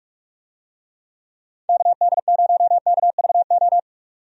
Morse code, heard as a keyed tone, about 43 wpm, 700 Hz. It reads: KD0KVO
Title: USA Call Signs - 6 Characters - 45wpm